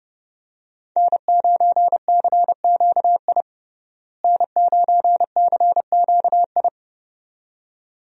Morse code, heard as a keyed tone, about 30 wpm, 700 Hz. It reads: N9CQI N9CQI
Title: USA Call Signs - 5 Characters - Repeated 2x - 30wpm